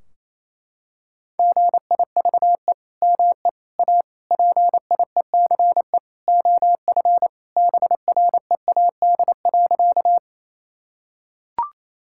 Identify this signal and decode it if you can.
Morse code, heard as a keyed tone, about 28 wpm, 700 Hz. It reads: GIVE ME A PIECE OF BREAD.
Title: Sentences from Top 1600 Words 28wpm